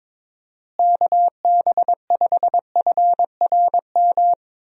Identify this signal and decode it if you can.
Morse code, heard as a keyed tone, about 22 wpm, 700 Hz. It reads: KB5FRM
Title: USA Call Signs - 6 Characters - Repeated 2x - 22wpm